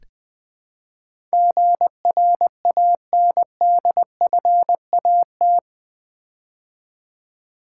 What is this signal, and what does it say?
Morse code, heard as a keyed tone, about 20 wpm, 700 Hz. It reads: GRANDFAT
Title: Build A Word - 4 Letters At A Time - 11-Letter Words - 20wpm